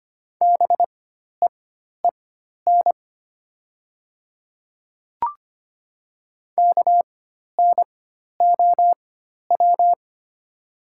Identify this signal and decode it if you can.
Morse code, heard as a keyed tone, about 25 wpm, spaced as for 12 wpm, 700 Hz. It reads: BEEN KNOW
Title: 4x Character Spacing - Top 100 Words - 25wpm